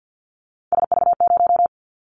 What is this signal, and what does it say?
Morse code, heard as a keyed tone, about 50 wpm, 700 Hz. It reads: SV0
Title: Callsign Prefixes for Top 50 DXCC - 50wpm